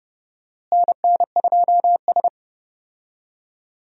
Morse code, heard as a keyed tone, about 30 wpm, 700 Hz. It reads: NN2S
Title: USA Call Signs - 4 Characters - Repeated 3x - 30wpm